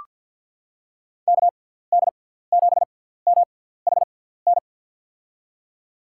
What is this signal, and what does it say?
Morse code, heard as a keyed tone, about 47 wpm, spaced as for 17 wpm, 700 Hz. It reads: KD7MHN
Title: USA Call Signs - 6 Characters - 6x Character Spacing - 50wpm